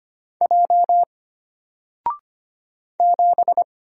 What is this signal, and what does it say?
Morse code, heard as a keyed tone, about 25 wpm, 700 Hz. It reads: J 7
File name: Single-Letter-Number-25wpm